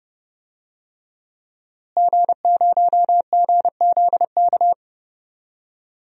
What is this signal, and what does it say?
Morse code, heard as a keyed tone, about 32 wpm, 700 Hz. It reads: G0GZK
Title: DXCC-100 Call Signs - 5 Characters - Repeated 3x - 30wpm